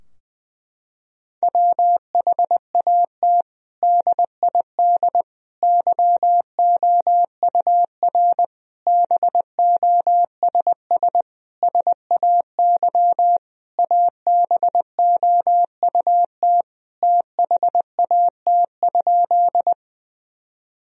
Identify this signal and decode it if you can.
Morse code, heard as a keyed tone, about 21 wpm, 700 Hz. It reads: WHAT DID YOUR BOSS SAY ABOUT THAT?
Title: Sentences from Top 1000 Words - Review Part 4 - 20wpm